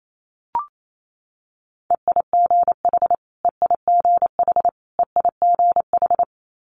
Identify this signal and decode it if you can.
Morse code, heard as a keyed tone, about 28 wpm, 700 Hz. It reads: EIGH EIGH EIGH
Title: Multi-letter Phonemes Repeated 3x 28wpm